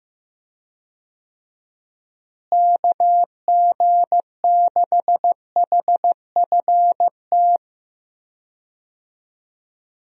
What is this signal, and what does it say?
Morse code, heard as a keyed tone, about 15 wpm, 700 Hz. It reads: KG6HFT
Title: USA Call Signs - 6 Characters - Repeated 3x - 15wpm